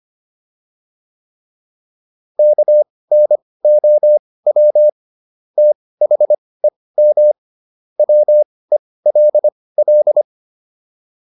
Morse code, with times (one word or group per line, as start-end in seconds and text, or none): 2.4-4.9 KNOW
5.6-7.3 THEM
8.0-10.2 WELL